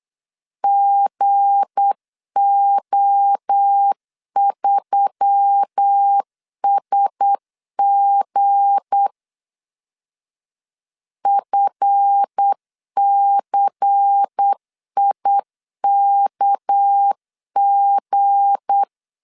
Morse code, heard as a keyed tone, about 8 wpm, 800 Hz. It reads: GO3SG FCIKG
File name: LEZIONI / 8 YJ45 / 8 YJ45 8wpm 3214